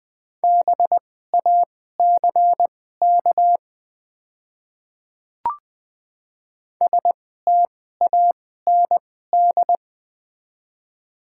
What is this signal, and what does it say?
Morse code, heard as a keyed tone, about 20 wpm, spaced as for 14 wpm, 700 Hz. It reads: BACK STAND
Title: 2x Character Spacing - Top 300 Words - 20wpm